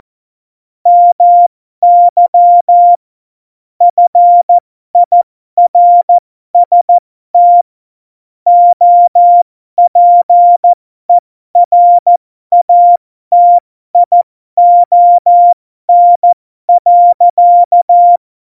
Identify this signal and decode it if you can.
Morse code, heard as a keyed tone, about 14 wpm, spaced as for 12 wpm, 700 Hz. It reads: MY FIRST OPERATION.
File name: Morsekurs / Witze